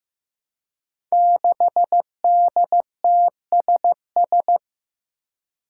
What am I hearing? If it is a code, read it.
Morse code, heard as a keyed tone, about 15 wpm, 700 Hz. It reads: 6DTSS
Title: Five Challenging Letter-Number Combinations 15wpm